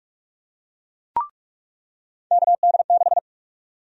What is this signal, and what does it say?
Morse code, heard as a keyed tone, about 46 wpm, 700 Hz. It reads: KD6
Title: USA Call Sign Prefixes - 45wpm